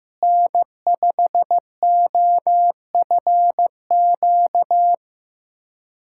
Morse code, heard as a keyed tone, about 15 wpm, 700 Hz. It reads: N5OFQ
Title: USA Callsign DE Callsign - 5 Characters - 15wpm